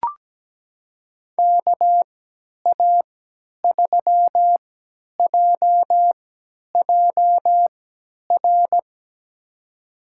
Morse code, heard as a keyed tone, about 17 wpm, spaced as for 10 wpm, 700 Hz. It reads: KA3JJR
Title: USA Call Signs - 6 Characters - 3x Character Spacing - 17wpm